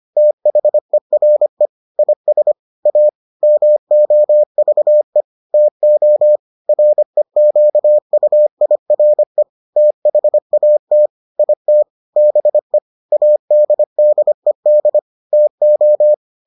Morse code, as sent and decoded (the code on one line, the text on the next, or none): - .... .-. / .. ... / .- / -- --- ...- . / - --- / .-. --.- ..- .. .-. / - .... .- - / .. - / -... / .- -.. -.. -.. / - ---
THERE IS A MOVE TO REQUIRE THAT IT BE ADDED TO